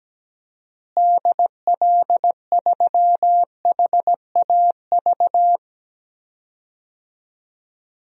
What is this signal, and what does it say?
Morse code, heard as a keyed tone, about 17 wpm, 700 Hz. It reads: DL3HAV
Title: DXCC-100 Call Signs - 6 Characters - Repeated 2x - 17wpm